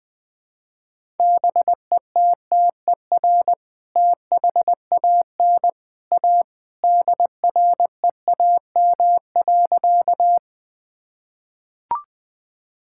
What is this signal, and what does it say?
Morse code, heard as a keyed tone, about 20 wpm, 700 Hz. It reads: BETTER THAN A DREAM.
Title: Sentences with 2-Word Phrases - 20wpm